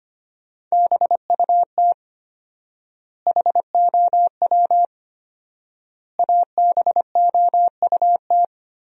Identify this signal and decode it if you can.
Morse code, heard as a keyed tone, about 25 wpm, 700 Hz. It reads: BUT HOW ABOUT
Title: 4x Word-Spacing - Sentences from Top 100 Words - 25wpm